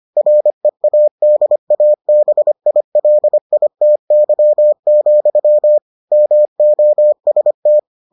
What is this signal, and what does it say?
Morse code, heard as a keyed tone, about 25 wpm, 600 Hz. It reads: READABILITY, MOST